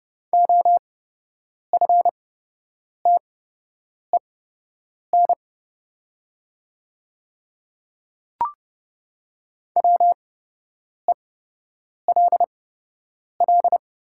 Morse code, heard as a keyed tone, about 30 wpm, spaced as for 8 wpm, 700 Hz. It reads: OFTEN WELL
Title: 8x Character Spacing - Top 500 Words - 30wpm